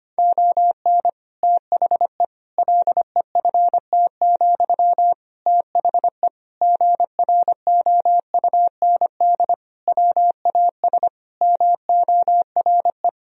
Morse code, heard as a keyed tone, about 25 wpm, 700 Hz. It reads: ON THE LEFT, THE GROUND WAS MORE
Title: Sentences from Top 1000 Words - Review Part 4 - 25wpm